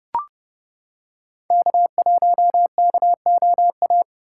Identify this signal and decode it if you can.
Morse code, heard as a keyed tone, about 29 wpm, 700 Hz. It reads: K1KOA